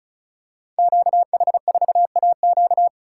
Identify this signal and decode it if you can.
Morse code, heard as a keyed tone, about 36 wpm, 700 Hz. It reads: QH4AQ
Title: Code Groups - 5 Letter:Numbers - 35wpm